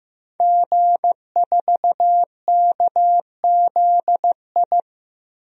Morse code, heard as a keyed tone, about 15 wpm, 700 Hz. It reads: G4KZI
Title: DXCC-100 Call Signs - 5 Characters - Repeated 2x - 15wpm